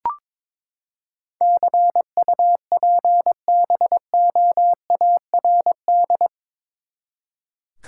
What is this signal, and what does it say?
Morse code, heard as a keyed tone, about 21 wpm, 700 Hz. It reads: CUPBOARD
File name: practice / 22wpm / Eight Letter Words - 22wpm